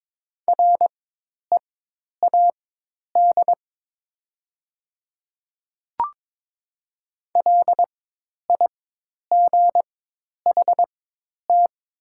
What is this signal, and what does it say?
Morse code, heard as a keyed tone, about 22 wpm, spaced as for 10 wpm, 700 Hz. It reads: READ LIGHT
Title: 4x Character Spacing - Top 200 Words - 22wpm